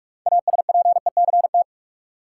Morse code, heard as a keyed tone, about 46 wpm, 700 Hz. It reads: ASPECT
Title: Six Letter Words - 45wpm